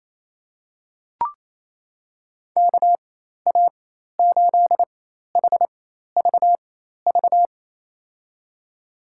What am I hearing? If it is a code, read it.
Morse code, heard as a keyed tone, about 27 wpm, spaced as for 13 wpm, 700 Hz. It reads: KA8HVV
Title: USA Call Signs - 6 Characters - 4x Character Spacing - 28wpm